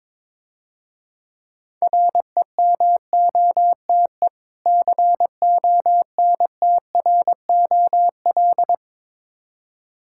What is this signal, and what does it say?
Morse code, heard as a keyed tone, about 22 wpm, 690 Hz. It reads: REMOTE CONTROL
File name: Easy Words - Speed-Racing with Hints - 22wpm